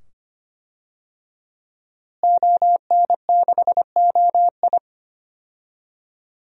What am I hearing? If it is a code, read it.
Morse code, heard as a keyed tone, about 25 wpm, 700 Hz. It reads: ON6OI
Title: DXCC-100 Call Signs - 5 Characters - Repeated 2x - 25wpm